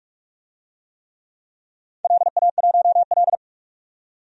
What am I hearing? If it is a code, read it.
Morse code, heard as a keyed tone, about 44 wpm, 700 Hz. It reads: LA1L
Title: DXCC-100 Call Signs - 4 Characters - 45wpm